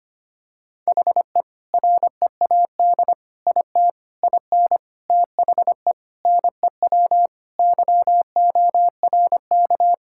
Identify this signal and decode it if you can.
Morse code, heard as a keyed tone, about 25 wpm, 700 Hz. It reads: HE READ IT IN THE NEW YORK
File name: practice / 25wpm / Sentences with 4-Word Phrases - 25wpm